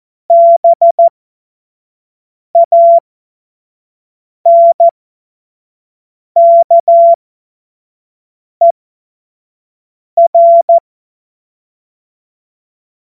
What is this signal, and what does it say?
Morse code, heard as a keyed tone, about 14 wpm, spaced as for 5 wpm, 680 Hz. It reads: BANKER
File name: Morsekurs / Witze / Witz 04. April 2023 25 F70